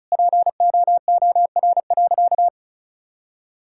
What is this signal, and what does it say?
Morse code, heard as a keyed tone, about 36 wpm, 700 Hz. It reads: POOR.
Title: Sentences from Top 1000 Words - Review Part 6 - 35wpm